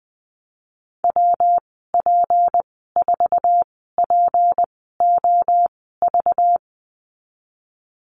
Morse code, heard as a keyed tone, about 20 wpm, spaced as for 15 wpm, 700 Hz. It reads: WP4POV
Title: 193 Lesson 31g - Letter v - Call Signs 20wpm